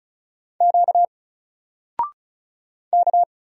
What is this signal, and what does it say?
Morse code, heard as a keyed tone, about 36 wpm, 700 Hz. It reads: Q K
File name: Single-Letter-Number-35wpm